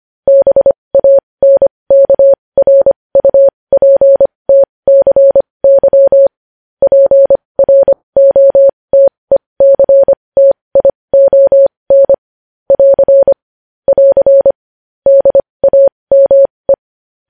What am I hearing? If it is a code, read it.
Morse code, heard as a keyed tone, about 25 wpm, spaced as for 20 wpm, 550 Hz. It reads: BANKRUPTCY PROTECTION <AR> <AR> DAME
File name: podcast / News headlines-1x-25x20